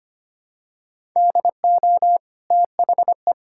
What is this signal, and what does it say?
Morse code, heard as a keyed tone, about 25 wpm, 700 Hz. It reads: DO THE
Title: Sentences from Top 1000 Words - Review Part 6 - 25wpm